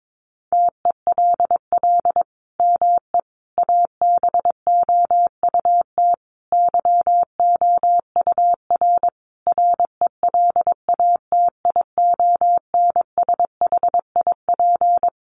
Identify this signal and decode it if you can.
Morse code, heard as a keyed tone, about 22 wpm, 700 Hz. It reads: TELL ME ABOUT YOUR RELATIONSHIP
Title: Sentences from Top 1100 Words 22wpm